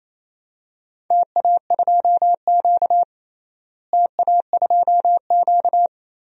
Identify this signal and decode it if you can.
Morse code, heard as a keyed tone, about 28 wpm, 700 Hz. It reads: TA2Q TA2Q
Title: DXCC-100 Call Signs - 4 Characters - Repeated 2x - 28wpm